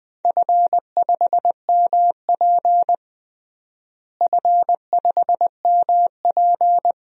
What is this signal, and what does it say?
Morse code, heard as a keyed tone, about 20 wpm, 700 Hz. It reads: F5MP F5MP
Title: DXCC-100 Call Signs - 4 Characters - Repeated 2x - 20wpm